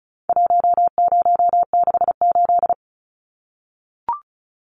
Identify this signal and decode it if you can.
Morse code, heard as a keyed tone, about 34 wpm, 700 Hz. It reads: 1068
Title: Numbers - 4-Digit 35wpm